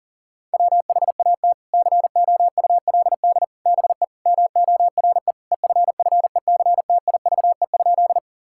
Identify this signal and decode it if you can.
Morse code, heard as a keyed tone, about 40 wpm, 700 Hz. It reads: WHAT COULD BE MORE EFFECTIVE?
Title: Sentences from Top 1000 Words - Review Part 6 - 40wpm